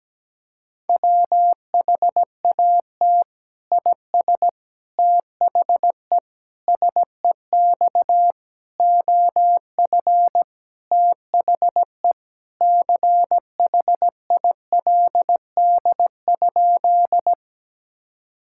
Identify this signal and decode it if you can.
Morse code, heard as a keyed tone, about 17 wpm, 700 Hz. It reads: WHAT IS THE SEX OF THE CHILD?
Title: Sentences from Top 1600 Words 17wpm